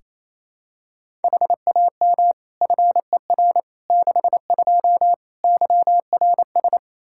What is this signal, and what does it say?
Morse code, heard as a keyed tone, about 27 wpm, 700 Hz. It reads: HAM FER 62 YRS
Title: K1USN SST QSO Practice 28wpm